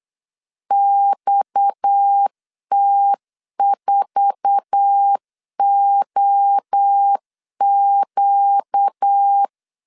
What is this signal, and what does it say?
Morse code, heard as a keyed tone, about 8 wpm, 800 Hz. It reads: XT4OQ
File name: CW / DET / 1 9 1500C S A 8wpm 5637